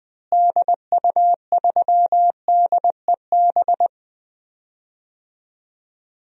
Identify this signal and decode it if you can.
Morse code, heard as a keyed tone, about 21 wpm, 700 Hz. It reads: DU3DEB
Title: DXCC-100 Call Signs - 6 Characters - Repeated 3x - 20wpm